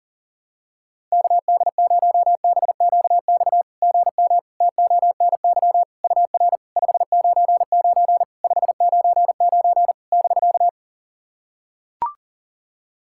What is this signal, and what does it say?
Morse code, heard as a keyed tone, about 40 wpm, 690 Hz. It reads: KD0BQX GM TONY UR 599 599 <BK>
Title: SOTA QSO Practice 40wpm